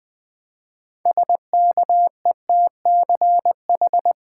Morse code, heard as a keyed tone, about 20 wpm, 700 Hz. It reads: SKETCH
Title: Build A Word - 3 Letters At A Time - 6-Letter Words - 20wpm